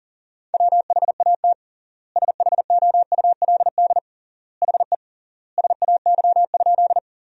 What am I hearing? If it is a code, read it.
Morse code, heard as a keyed tone, about 40 wpm, 700 Hz. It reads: WHAT SHOULD HE SAY?
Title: 3x Word-Spacing - Sentences from Top 300 Words - 40wpm